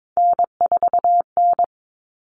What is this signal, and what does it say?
Morse code, heard as a keyed tone, about 22 wpm, 700 Hz. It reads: N4N
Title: Common Cut Numbers - 3-Digit 22wpm